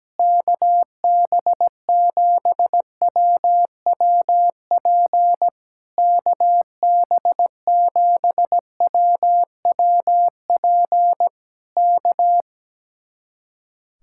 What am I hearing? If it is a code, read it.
Morse code, heard as a keyed tone, about 17 wpm, 700 Hz. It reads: KB7WWP KB7WWP K
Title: Speed-Racing Common QSO Phrases - 17wpm